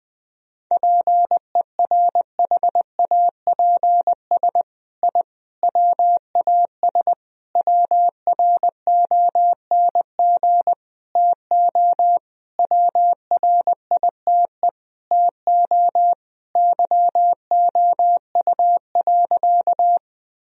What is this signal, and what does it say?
Morse code, heard as a keyed tone, about 21 wpm, 700 Hz. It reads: PERHAPS I WAS WRONG TO WRITE TO YOU.
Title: Sentences from Top 1000 Words - Review Part 1 - 20wpm